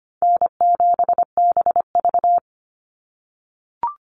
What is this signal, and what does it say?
Morse code, heard as a keyed tone, about 25 wpm, 700 Hz. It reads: N7BV
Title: Call Signs - 25wpm